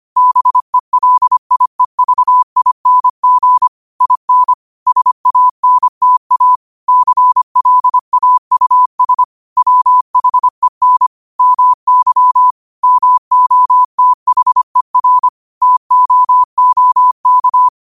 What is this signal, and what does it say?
Morse code, heard as a keyed tone, about 25 wpm, 1000 Hz. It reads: BELIEVING IN SANTA CLAUS WHEN MY MOTHER TOOK
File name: qotdfiles / 25wpm